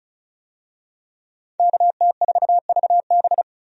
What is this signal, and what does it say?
Morse code, heard as a keyed tone, about 36 wpm, 700 Hz. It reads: KT4VB
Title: USA Call Signs - 5 Characters - 35wpm